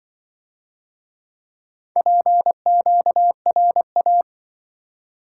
Morse code, heard as a keyed tone, about 25 wpm, 700 Hz. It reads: PQRA